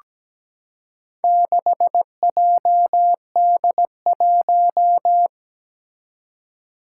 Morse code, heard as a keyed tone, about 17 wpm, 700 Hz. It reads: 6JD1
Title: Four Challenging Letter-Number Combinations 17wpm